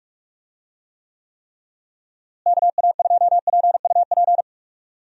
Morse code, heard as a keyed tone, about 46 wpm, 700 Hz. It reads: KA2PUP
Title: USA Call Signs - 6 Characters - Repeated 3x - 45wpm